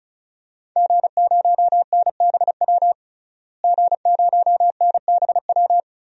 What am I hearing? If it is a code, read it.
Morse code, heard as a keyed tone, about 36 wpm, 700 Hz. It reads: G0NBW G0NBW
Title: DXCC-100 Call Signs - 5 Characters - Repeated 3x - 35wpm